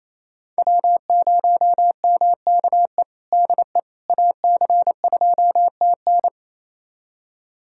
Morse code, heard as a keyed tone, about 28 wpm, 700 Hz. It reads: W0MKE DE AC2TN
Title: USA Callsign DE Callsign - 5 Characters - 28wpm